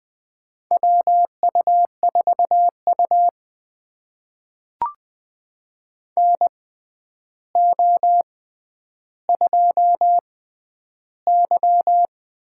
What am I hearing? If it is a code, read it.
Morse code, heard as a keyed tone, about 20 wpm, spaced as for 10 wpm, 700 Hz. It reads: WU4U NO2Y
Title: USA Call Signs - 4 Characters - 6x Character Spacing - 20wpm